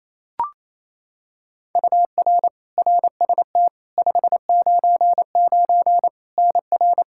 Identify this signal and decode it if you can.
Morse code, heard as a keyed tone, about 27 wpm, 700 Hz. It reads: UR RST 599 NR
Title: Practice QSO - Exchange 1 - Beginning - 28wpm